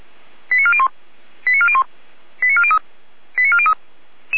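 Five-tone selective call (ZVEI), heard as quick runs of five tones, tones of about 80 ms. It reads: 89401 89401 89403 89403
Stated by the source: ZVEI five-tone call received by SDR, from the Feuerwehr Orlamünde fire brigade audio server.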